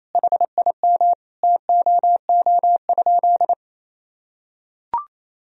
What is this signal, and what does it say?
Morse code, heard as a keyed tone, about 28 wpm, 700 Hz. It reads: HIM TOO?
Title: Sentences from Top 500 Words and Abbreviations 28wpm